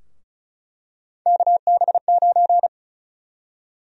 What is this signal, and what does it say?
Morse code, heard as a keyed tone, about 34 wpm, 700 Hz. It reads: KB9
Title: USA Call Sign Prefixes - Repeated 2x - 35wpm